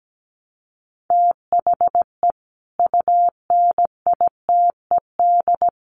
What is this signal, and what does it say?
Morse code, heard as a keyed tone, about 17 wpm, 700 Hz. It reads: THE UNITED
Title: Sets of 5 Words 17wpm